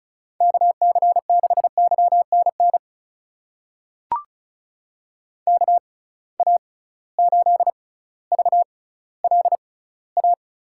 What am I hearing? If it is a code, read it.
Morse code, heard as a keyed tone, about 34 wpm, spaced as for 18 wpm, 700 Hz. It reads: KC6YNN KA8VLA
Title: USA Call Signs - 6 Characters - 6x Character Spacing - 35wpm